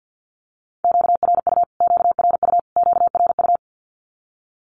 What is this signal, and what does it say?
Morse code, heard as a keyed tone, about 48 wpm, 700 Hz. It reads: QRU QRU QRU
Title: Q-Codes Repeated 3x 50wpm